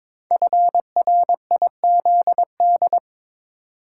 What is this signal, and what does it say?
Morse code, heard as a keyed tone, about 22 wpm, 700 Hz. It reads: FRIZD
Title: Code Groups - 5 Letter:Numbers - 22wpm